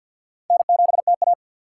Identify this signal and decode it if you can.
Morse code, heard as a keyed tone, about 53 wpm, 700 Hz. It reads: N7TA
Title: USA Call Signs - 2x Character Spacing - 50wpm